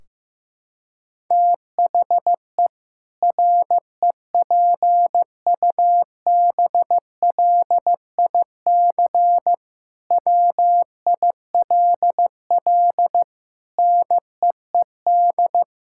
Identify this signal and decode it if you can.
Morse code, heard as a keyed tone, about 16 wpm, 700 Hz. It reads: THE REPUBLIC WILL NEED
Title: Sentences from Top 1600 Words 15wpm